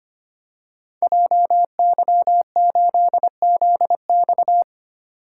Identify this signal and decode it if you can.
Morse code, heard as a keyed tone, about 25 wpm, 700 Hz. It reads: JY8ZX
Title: Code Groups - 5 Letter:Numbers - 25wpm